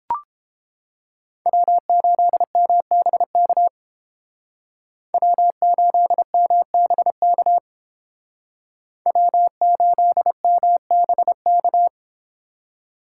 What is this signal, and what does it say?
Morse code, heard as a keyed tone, about 30 wpm, 700 Hz. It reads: W8MBK W8MBK W8MBK